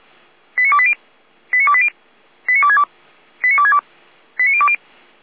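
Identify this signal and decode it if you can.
Five-tone selective call (ZVEI), heard as quick runs of five tones, tones of about 80 ms. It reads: 89280 89280 89262 89262 89020
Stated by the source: ZVEI five-tone call received by SDR.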